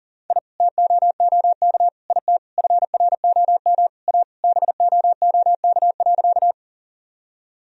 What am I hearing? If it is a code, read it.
Morse code, heard as a keyed tone, about 37 wpm, 700 Hz. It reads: I TOOK IT FROM A BOOK.
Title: Sentences from Top 1000 Words - Review Part 1 - 40wpm